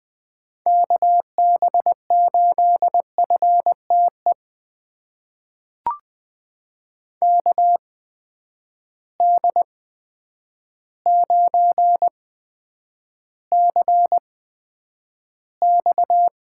Morse code, heard as a keyed tone, about 21 wpm, spaced as for 9 wpm, 700 Hz. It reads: KB8FTE KD9CX
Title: USA Call Signs - 6 Characters - 8x Character Spacing - 20wpm